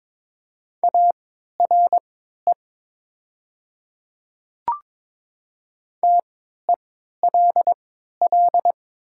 Morse code, heard as a keyed tone, about 22 wpm, spaced as for 13 wpm, 700 Hz. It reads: ARE TELL
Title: 3x Character Spacing - Top 300 Words - 22wpm